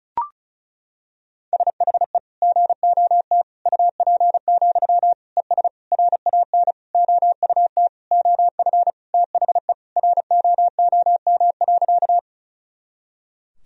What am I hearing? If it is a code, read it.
Morse code, heard as a keyed tone, about 34 wpm, 700 Hz. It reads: SHE GOT UP, ES RAN OUT OF THE ROOM.
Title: Sentences from Top 500 Words and Abbreviations 35wpm